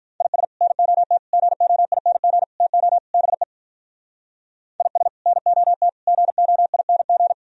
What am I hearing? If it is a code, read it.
Morse code, heard as a keyed tone, about 42 wpm, 700 Hz. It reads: IS NOT GOING TO BE IS NOT GOING
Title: Speed-Racing - Sets of 5 Words 35wpm